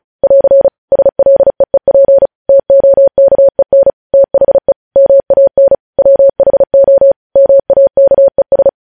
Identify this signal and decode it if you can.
Morse code, heard as a keyed tone, about 36 wpm, 550 Hz. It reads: <AR> SLEEP TOKEN THE MAN WHO MAKES
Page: news headlines in morse code at 35 wpm